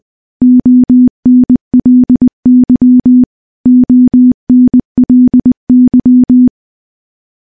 Morse code, heard as a keyed tone, about 20 wpm, 250 Hz. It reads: ONLY ONLY
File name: morse-training-20WPM